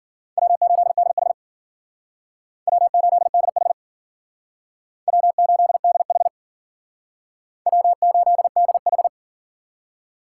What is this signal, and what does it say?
Morse code, heard as a keyed tone, about 50 wpm, 700 Hz. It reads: W8DH W8DH W8DH W8DH
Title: Speed Racing - USA Call Signs 40wpm